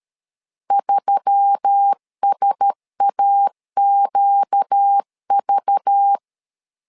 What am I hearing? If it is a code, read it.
Morse code, heard as a keyed tone, about 13 wpm, 800 Hz. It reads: 3SAQV
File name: LEZIONI / 5 LBVC / 5 LBV 12wpm 1855